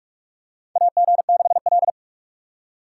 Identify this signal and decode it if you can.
Morse code, heard as a keyed tone, about 43 wpm, 700 Hz. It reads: AG6L
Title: USA Call Signs - 4 Characters - 3x Character Spacing - 45wpm